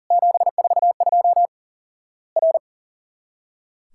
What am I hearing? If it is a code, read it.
Morse code, heard as a keyed tone, about 41 wpm, 700 Hz. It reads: 742 R
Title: Worked All Europe DX Contest - QTC Exchange - 40wpm